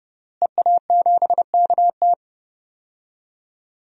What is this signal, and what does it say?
Morse code, heard as a keyed tone, about 30 wpm, 700 Hz. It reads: EA7KT
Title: DXCC-100 Call Signs - 5 Characters - Repeated 3x - 30wpm